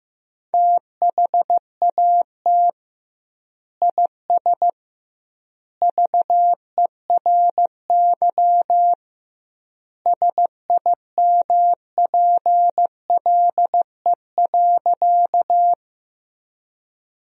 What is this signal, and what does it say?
Morse code, heard as a keyed tone, about 15 wpm, 700 Hz. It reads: THAT IS VERY SIMPLE.
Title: 2x Word-Spacing - Sentences from Top 400 Words - 15wpm